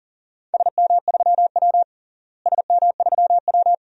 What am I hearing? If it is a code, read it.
Morse code, heard as a keyed tone, about 40 wpm, 700 Hz. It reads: SM3W SM3W
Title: DXCC-100 Call Signs - 4 Characters - Repeated 2x - 40wpm